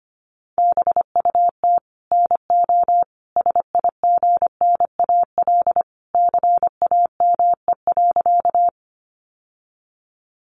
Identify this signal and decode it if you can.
Morse code, heard as a keyed tone, about 25 wpm, 700 Hz. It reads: BUT NO SIGNAL CAME.
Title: Sentences from Top 1400 Words 25wpm